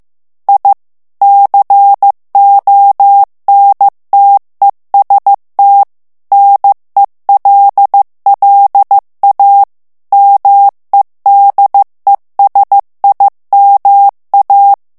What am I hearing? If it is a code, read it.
Morse code, heard as a keyed tone, about 15 wpm, 800 Hz. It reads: I CONTEST NELLA MEDESIMA
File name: morse mp3 / ITALIEN texte clair 2 vitesse 14 mots minute